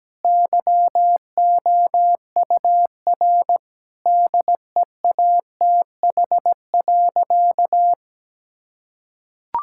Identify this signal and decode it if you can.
Morse code, heard as a keyed tone, about 17 wpm, 700 Hz. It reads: YOUR DEATH.
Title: Sentences from Top 1600 Words 17wpm